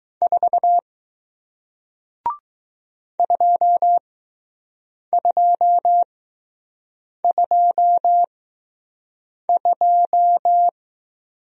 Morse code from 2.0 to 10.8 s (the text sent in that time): 2222